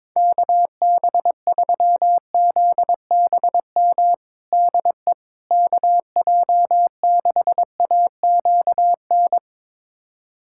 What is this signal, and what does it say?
Morse code, heard as a keyed tone, about 21 wpm, 700 Hz. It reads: KB3ZBM DE KJ6AQN
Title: USA Callsign DE Callsign - 6 Characters - 22wpm